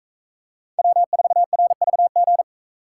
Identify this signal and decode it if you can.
Morse code, heard as a keyed tone, about 40 wpm, 700 Hz. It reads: W4RVG